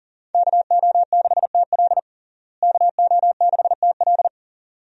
Morse code, heard as a keyed tone, about 40 wpm, 700 Hz. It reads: KO6TL KO6TL
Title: USA Call Signs - 5 Characters - Repeated 2x - 40wpm